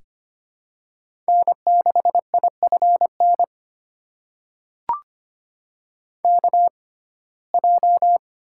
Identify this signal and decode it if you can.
Morse code, heard as a keyed tone, about 25 wpm, spaced as for 18 wpm, 700 Hz. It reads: N6IFN KJ
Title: USA Call Signs - 6x Character Spacing - 25wpm